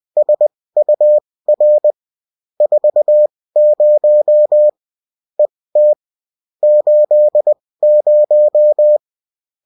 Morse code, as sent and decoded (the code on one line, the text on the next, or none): ... ..- .-. / ....- ----- / . - / ---.. -----
SUR 40 ET 80